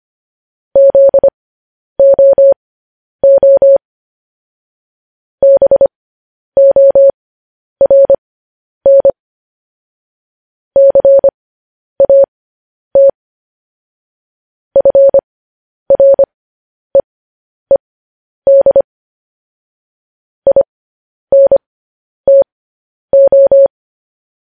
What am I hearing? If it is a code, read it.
Morse code, heard as a keyed tone, about 25 wpm, spaced as for 10 wpm, 550 Hz. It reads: ZOO BORN CAT FREED INTO
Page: news-headlines-in-morse-code-at-10-wpm